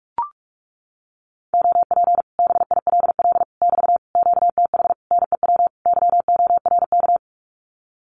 Morse code, heard as a keyed tone, about 44 wpm, 700 Hz. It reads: OP BILL <BT> QTH NEW YORK
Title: Common QSO Phrases Encore 45wpm